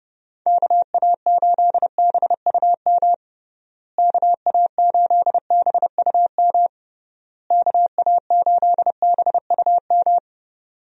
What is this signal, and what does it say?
Morse code, heard as a keyed tone, about 31 wpm, 700 Hz. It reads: KA8BUM KA8BUM KA8BUM
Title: USA Call Signs - 6 Characters - Repeated 3x - 30wpm